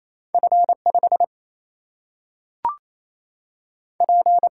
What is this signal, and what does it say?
Morse code, heard as a keyed tone, about 28 wpm, 700 Hz. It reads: F 5 P
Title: Single-Letter Single-Number 28wpm